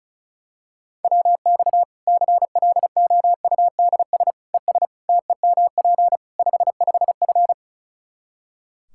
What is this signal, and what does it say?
Morse code, heard as a keyed tone, about 36 wpm, 700 Hz. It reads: WX CLOUDS ES TEMP 55F
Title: K1USN SST QSO Practice 35wpm